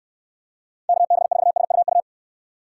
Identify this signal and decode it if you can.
Morse code, heard as a keyed tone, about 69 wpm, 700 Hz. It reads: KB4SUU